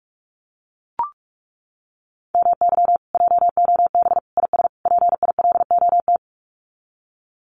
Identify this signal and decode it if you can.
Morse code, heard as a keyed tone, about 44 wpm, 700 Hz. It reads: MY JOB IS PILOT
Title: Common QSO Phrases Encore 45wpm